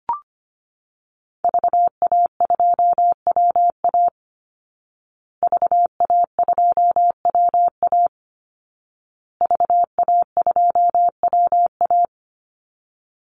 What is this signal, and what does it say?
Morse code, heard as a keyed tone, about 25 wpm, 700 Hz. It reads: VA2WA VA2WA VA2WA
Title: Call Signs Repeated 2x - 25wpm